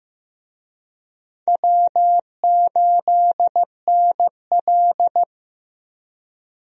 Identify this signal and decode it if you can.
Morse code, heard as a keyed tone, about 15 wpm, 700 Hz. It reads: W8NL
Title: USA Call Signs - 4 Characters - Repeated 3x - 15wpm